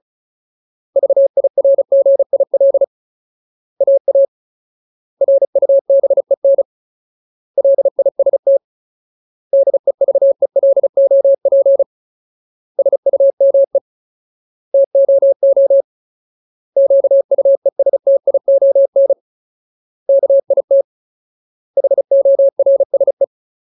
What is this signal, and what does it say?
Morse code, heard as a keyed tone, about 34 wpm, 550 Hz. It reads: VIRGIL AA RUBEN LIST DEVELOP SUME TOO QUESTION KIT HORSE
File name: Ultimate Word List / 35WPM Ultimate